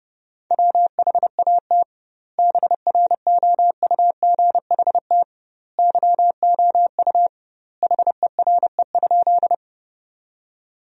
Morse code, heard as a keyed tone, about 32 wpm, 700 Hz. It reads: WHAT BROUGHT YOU HERE?
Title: 2x Word-Spacing - Sentences from Top 500 Words - 30wpm